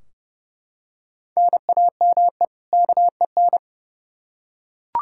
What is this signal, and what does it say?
Morse code, heard as a keyed tone, about 30 wpm, 700 Hz. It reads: NAME KEN
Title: Practice QSO - Exchange 1 - Beginning - 30wpm